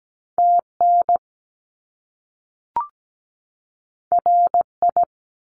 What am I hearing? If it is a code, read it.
Morse code, heard as a keyed tone, about 17 wpm, 700 Hz. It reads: TN RI
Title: US States 17wpm